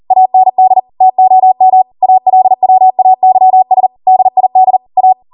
Morse code, heard as a keyed tone, about 40 wpm, 750 Hz. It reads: AND TOM ALWAYS DID A